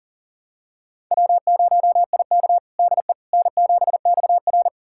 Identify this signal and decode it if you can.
Morse code, heard as a keyed tone, about 40 wpm, 700 Hz. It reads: W0IK DE N7XR
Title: USA Callsign DE Callsign - 4 Characters - 40wpm